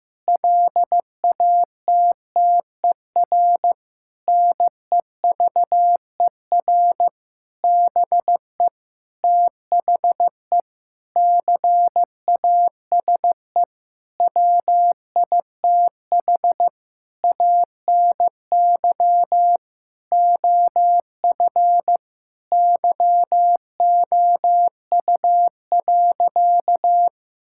Morse code, heard as a keyed tone, about 15 wpm, 700 Hz. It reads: LATTER NEVER BE THE CASE WITH ANY OF YOU.
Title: Sentences from Top 1600 Words 15wpm